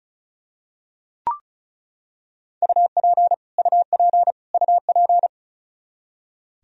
Morse code, heard as a keyed tone, about 36 wpm, 700 Hz. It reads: UP UP UP
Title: Top 100 Words Repeated 35wpm